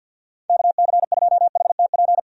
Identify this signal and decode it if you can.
Morse code, heard as a keyed tone, about 45 wpm, 700 Hz. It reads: KC2HTP
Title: USA Call Signs - 6 Characters - 50wpm